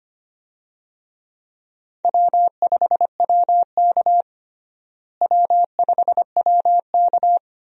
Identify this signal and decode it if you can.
Morse code, heard as a keyed tone, about 25 wpm, 700 Hz. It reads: W5WK W5WK
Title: USA Call Signs - 4 Characters - Repeated 2x - 25wpm